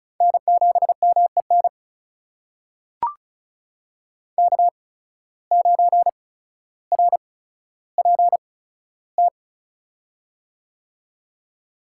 Morse code, heard as a keyed tone, about 36 wpm, spaced as for 15 wpm, 700 Hz. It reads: N7MEN K9RPT
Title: USA Call Signs - 5 Characters - 8x Character Spacing - 35wpm